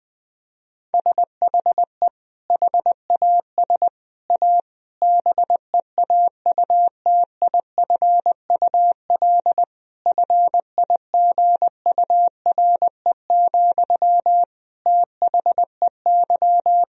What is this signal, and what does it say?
Morse code, heard as a keyed tone, about 20 wpm, 700 Hz. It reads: SHE HAS A BEAUTIFUL FIGURE, THEY
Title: Sentences from Top 1000 Words - Review Part 2 - 20wpm